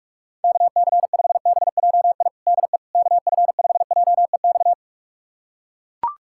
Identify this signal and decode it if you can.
Morse code, heard as a keyed tone, about 41 wpm, 700 Hz. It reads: KC5BJI DE KF5JEX
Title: USA Callsign DE Callsign - 6 Characters - 45wpm